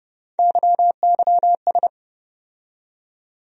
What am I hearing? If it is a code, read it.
Morse code, heard as a keyed tone, about 30 wpm, 700 Hz. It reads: YYS
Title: USA Call Sign Postfixes - 3 Letters 30wpm